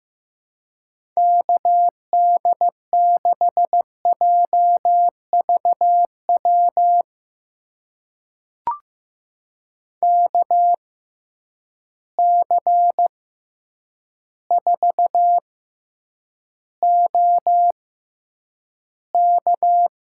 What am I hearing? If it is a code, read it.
Morse code, heard as a keyed tone, about 15 wpm, spaced as for 8 wpm, 700 Hz. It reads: KD6JVW KC4OK
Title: USA Call Signs - 6x Character Spacing - 15wpm